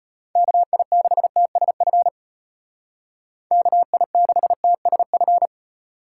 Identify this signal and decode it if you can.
Morse code, heard as a keyed tone, about 31 wpm, 700 Hz. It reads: KI6TSF KI6TSF